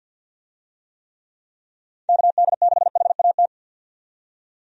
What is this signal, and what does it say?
Morse code, heard as a keyed tone, about 52 wpm, 700 Hz. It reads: KD6HAT
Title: USA Call Signs - 6 Characters - 50wpm